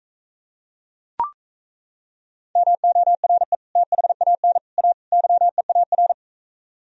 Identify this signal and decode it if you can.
Morse code, heard as a keyed tone, about 34 wpm, 690 Hz. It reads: MORE THAN A YEAR
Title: Speed-Racing - Sets of 4 Words 28wpm